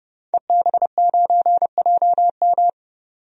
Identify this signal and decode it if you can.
Morse code, heard as a keyed tone, about 30 wpm, 700 Hz. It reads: EB9JM
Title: DXCC-100 Call Signs - 5 Characters - 30wpm